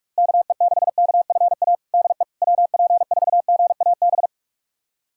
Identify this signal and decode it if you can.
Morse code, heard as a keyed tone, about 46 wpm, 700 Hz. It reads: KE6KFA DE WP4GAB
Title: USA Callsign DE Callsign - 6 Characters - 45wpm